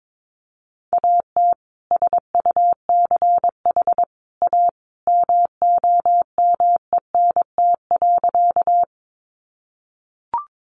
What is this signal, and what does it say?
Morse code, heard as a keyed tone, about 22 wpm, 700 Hz. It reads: AT SUCH A MOMENT.
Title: Sentences from Top 1000 Words 22wpm